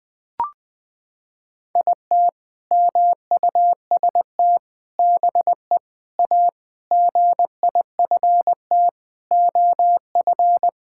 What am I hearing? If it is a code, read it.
Morse code, heard as a keyed tone, about 20 wpm, 700 Hz. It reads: IT MUST BE A GIFT OF
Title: Sentences from Top 1000 Words - Review Part 6 - 20wpm